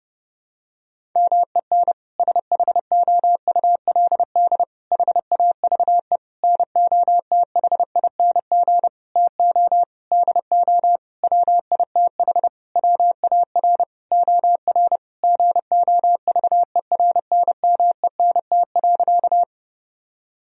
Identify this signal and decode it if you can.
Morse code, heard as a keyed tone, about 32 wpm, 700 Hz. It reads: MEN SHOULD HAVE NOTHING TO DO WITH WAR OR GOVERNMENT.
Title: Sentences from Top 1000 Words - Review Part 3 - 30wpm